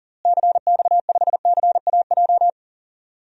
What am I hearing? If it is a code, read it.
Morse code, heard as a keyed tone, about 40 wpm, 700 Hz. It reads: CX5CAJ
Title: DXCC-100 Call Signs - 6 Characters - Repeated 2x - 40wpm